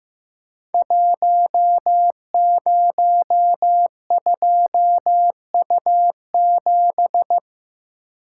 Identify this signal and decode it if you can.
Morse code, heard as a keyed tone, about 15 wpm, 700 Hz. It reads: 102U7
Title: Code Groups - 5 Letter:Numbers - 15wpm